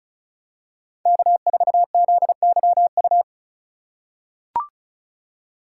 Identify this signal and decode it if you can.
Morse code, heard as a keyed tone, about 34 wpm, 700 Hz. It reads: K4ZYU
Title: USA Call Signs - 4x Character Spacing - 35wpm